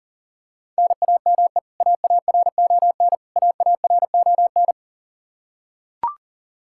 Morse code, heard as a keyed tone, about 40 wpm, 700 Hz. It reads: NAME AARON AARON
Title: Practice QSO - Exchange 1 - Beginning - 40wpm